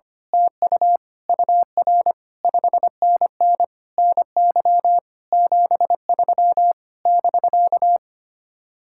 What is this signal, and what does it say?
Morse code, heard as a keyed tone, about 24 wpm, 700 Hz. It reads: TU UR 5NN NY 73 <BK>
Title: Speed-Racing Common QSO Phrases - 25wpm